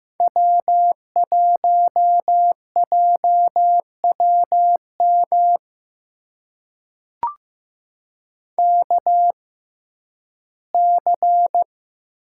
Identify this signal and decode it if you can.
Morse code, heard as a keyed tone, about 15 wpm, spaced as for 11 wpm, 700 Hz. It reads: W1JWM KC
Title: USA Call Signs - 5 Characters - 6x Character Spacing - 15wpm